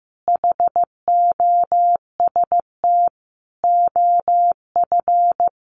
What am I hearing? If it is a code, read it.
Morse code, heard as a keyed tone, about 15 wpm, 700 Hz. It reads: HOST OF
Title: Sets of 5 Words Encore2 15wpm